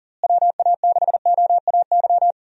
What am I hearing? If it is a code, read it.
Morse code, heard as a keyed tone, about 37 wpm, 700 Hz. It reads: WA6OAY
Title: USA Call Signs - 6 Characters - Repeated 2x - 40wpm